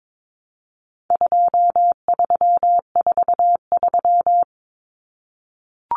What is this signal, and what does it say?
Morse code, heard as a keyed tone, about 22 wpm, 700 Hz. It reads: 2343
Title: Numbers - 4-Digit 22wpm